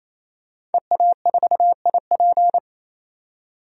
Morse code, heard as a keyed tone, about 28 wpm, 700 Hz. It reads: EA4IP
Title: DXCC-100 Call Signs - 5 Characters - 28wpm